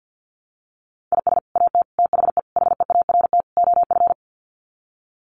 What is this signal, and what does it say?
Morse code, heard as a keyed tone, about 50 wpm, 700 Hz. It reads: IS AT THE HEART OF
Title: Sets of 5 Words Encore2 50wpm